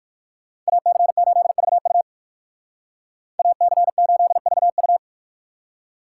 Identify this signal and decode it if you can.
Morse code, heard as a keyed tone, about 56 wpm, 700 Hz. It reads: AC8VU AC8VU
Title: Speed Racing - USA 2x2 Call Signs 45wpm